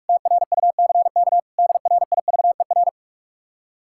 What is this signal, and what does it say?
Morse code, heard as a keyed tone, about 62 wpm, spaced as for 53 wpm, 700 Hz. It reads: TRUCK DRIVER
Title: Speed-Racing Common QSO Phrases Encore - 45wpm